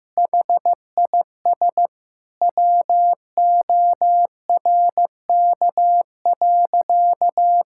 Morse code, heard as a keyed tone, about 15 wpm, 700 Hz. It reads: HIS WORK.
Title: Sentences from Top 1000 Words - Review Part 6 - 15wpm